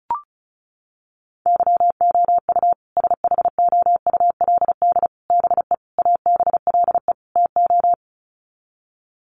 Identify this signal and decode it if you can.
Morse code, heard as a keyed tone, about 34 wpm, 700 Hz. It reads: YOU SHOULD BE ABLE TO
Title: Sets of 5 Words Encore 35wpm